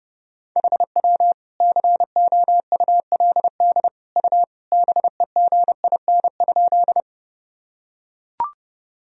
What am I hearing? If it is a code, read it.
Morse code, heard as a keyed tone, about 30 wpm, 700 Hz. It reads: HW COULD U BEGIN?
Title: Sentences from Top 300 Words and Abbreviations 30wpm